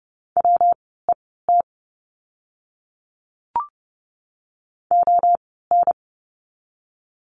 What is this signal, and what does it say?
Morse code, heard as a keyed tone, about 30 wpm, spaced as for 17 wpm, 700 Hz. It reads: WET ON
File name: course / 30:10wpm / 115 Lesson 19f - Letter w - Words 30wpm